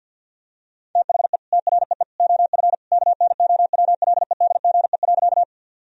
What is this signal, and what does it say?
Morse code, heard as a keyed tone, about 50 wpm, 700 Hz. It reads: THE TREE OF KNOWLEDGE.
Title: Sentences from Top 1000 Words - Review Part 3 - 50wpm